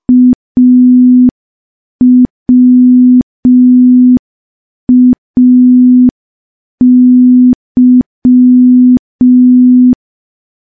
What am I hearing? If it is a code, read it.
Morse code, heard as a keyed tone, about 5 wpm, 250 Hz. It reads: AWAY
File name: morse / morse-training-05WPM